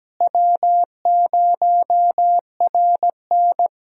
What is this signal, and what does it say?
Morse code, heard as a keyed tone, about 17 wpm, 700 Hz. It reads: W0RN
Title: USA Call Signs - 4 Characters - Repeated 3x - 17wpm